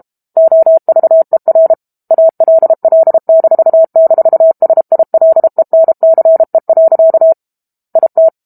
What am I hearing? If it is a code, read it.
Morse code, heard as a keyed tone, about 32 wpm, 650 Hz. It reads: OVER ALL--SILENCE. IT
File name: Book Two / 06-Chapter 6 - 32 WPM